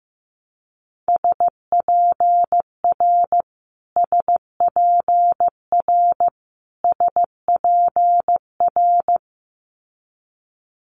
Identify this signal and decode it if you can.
Morse code, heard as a keyed tone, about 15 wpm, 700 Hz. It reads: SPR SPR SPR
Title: Multi-letter Phonemes Repeated 3x 15wpm